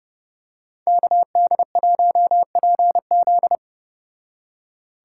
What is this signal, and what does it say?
Morse code, heard as a keyed tone, about 30 wpm, 700 Hz. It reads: KD1PZ
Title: Speed Racing - USA 2x2 Call Signs 25wpm